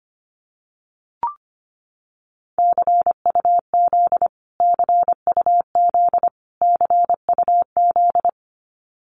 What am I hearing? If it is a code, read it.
Morse code, heard as a keyed tone, about 25 wpm, 700 Hz. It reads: CUZ CUZ CUZ
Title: Abbreviations Repeated 3x - 25wpm